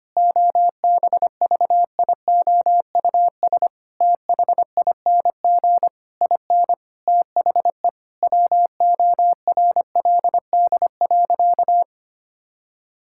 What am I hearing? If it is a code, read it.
Morse code, heard as a keyed tone, about 25 wpm, 700 Hz. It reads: OBVIOUS THING IN THE WORLD.
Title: Sentences from Top 1600 Words 25wpm